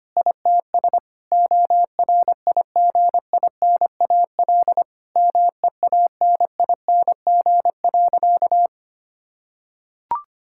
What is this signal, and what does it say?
Morse code, heard as a keyed tone, about 25 wpm, 700 Hz. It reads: ITS ORIGINAL MEANING.
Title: Sentences from Top 1000 Words - Review Part 6 - 25wpm